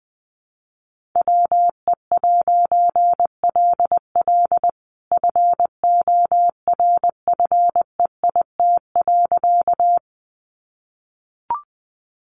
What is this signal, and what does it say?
Morse code, heard as a keyed tone, about 20 wpm, 700 Hz. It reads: WE'LL FORFEIT.
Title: A Sound Of Thunder 20wpm